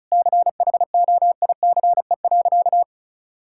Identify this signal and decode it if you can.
Morse code, heard as a keyed tone, about 36 wpm, 700 Hz. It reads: CHOICE.
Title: Sentences from Top 1000 Words - Review Part 6 - 35wpm